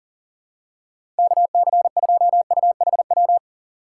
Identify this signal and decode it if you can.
Morse code, heard as a keyed tone, about 40 wpm, 700 Hz. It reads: KC2UHW
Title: USA Call Signs - 8x Character Spacing - 40wpm